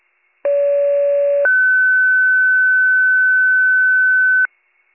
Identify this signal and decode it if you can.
Two-tone page, heard as a one-second tone, then a three-second tone, about 570 then 1500 Hz. It is heard from 0.4 to 4.5 s.